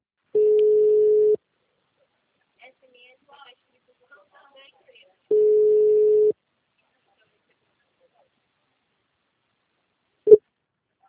Ringback tone (Europe, 425 Hz).